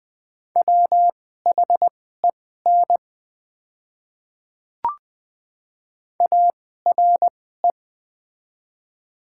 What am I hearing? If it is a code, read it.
Morse code, heard as a keyed tone, about 20 wpm, spaced as for 14 wpm, 700 Hz. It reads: WHEN ARE
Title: 2x Character Spacing - Top 100 Words - 20wpm